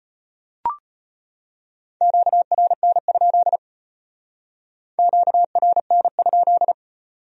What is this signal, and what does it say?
Morse code, heard as a keyed tone, about 33 wpm, 700 Hz. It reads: QRN? QRN?